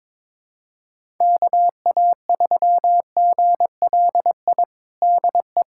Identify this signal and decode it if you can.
Morse code, heard as a keyed tone, about 22 wpm, 700 Hz. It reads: KA3GLI DE
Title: USA Callsign DE Callsign - 6 Characters - 22wpm